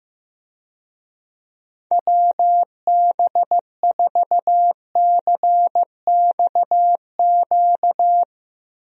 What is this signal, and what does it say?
Morse code, heard as a keyed tone, about 15 wpm, 700 Hz. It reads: WB4CXQ
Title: USA Call Signs - 6 Characters - Repeated 2x - 15wpm